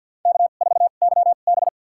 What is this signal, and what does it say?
Morse code, heard as a keyed tone, about 52 wpm, spaced as for 36 wpm, 700 Hz. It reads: K4YB
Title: USA Call Signs - 2x Character Spacing - 50wpm